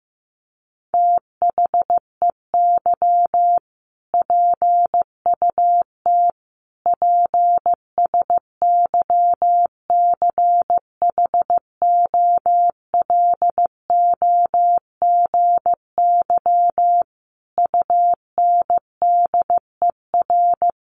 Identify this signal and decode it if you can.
Morse code, heard as a keyed tone, about 15 wpm, 700 Hz. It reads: THEY PUT PSYCHOLOGY UNDER
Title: Sentences from Top 1400 Words 15wpm